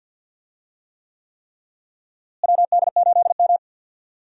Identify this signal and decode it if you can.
Morse code, heard as a keyed tone, about 53 wpm, 700 Hz. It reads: WD8M
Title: USA Call Signs - 4 Characters - Repeated 3x - 50wpm